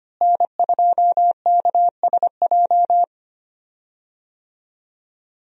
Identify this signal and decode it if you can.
Morse code, heard as a keyed tone, about 25 wpm, 700 Hz. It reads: N2KSJ
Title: USA Call Signs - 5 Characters - Repeated 3x - 25wpm